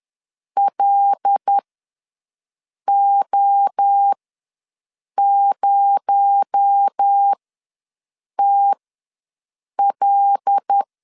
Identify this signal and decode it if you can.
Morse code, heard as a keyed tone, about 10 wpm, spaced as for 6 wpm, 800 Hz. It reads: LO0TL
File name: LEZIONI / 7 GF9Z / 7 GF9 ws 10wpm 4034